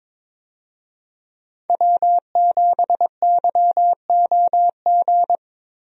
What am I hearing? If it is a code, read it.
Morse code, heard as a keyed tone, about 22 wpm, 690 Hz. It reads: W7YOG
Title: USA Call Signs - 5 Characters - Repeated 3x - 22wpm